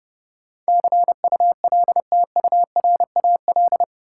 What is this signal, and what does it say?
Morse code, heard as a keyed tone, about 31 wpm, 700 Hz. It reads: CULTURAL
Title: Sentences from Top 1500 Words - New Word Drill - 30wpm